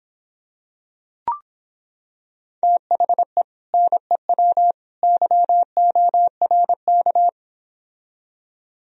Morse code, heard as a keyed tone, about 26 wpm, 700 Hz. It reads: THE NEW YORK